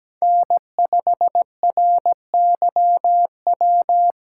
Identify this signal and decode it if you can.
Morse code, heard as a keyed tone, about 17 wpm, 700 Hz. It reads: N5RYW
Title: USA Call Signs - 5 Characters - 2x Character Spacing - 17wpm